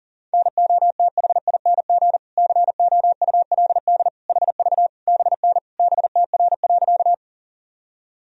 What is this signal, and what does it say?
Morse code, heard as a keyed tone, about 40 wpm, 700 Hz. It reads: NOTHING COULD HV BN BTR.
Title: Sentences from Top 500 Words and Abbreviations 40wpm